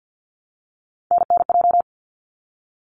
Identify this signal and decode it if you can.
Morse code, heard as a keyed tone, about 48 wpm, 700 Hz. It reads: NNJ
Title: ARRL Sections 50wpm